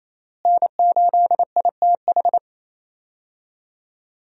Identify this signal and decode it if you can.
Morse code, heard as a keyed tone, about 28 wpm, 700 Hz. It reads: N8ITH
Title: USA Call Signs - 5 Characters - Repeated 3x - 28wpm